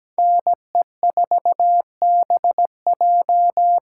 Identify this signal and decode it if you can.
Morse code, heard as a keyed tone, about 17 wpm, 700 Hz. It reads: NE4BJ
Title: USA Callsign DE Callsign - 5 Characters - 17wpm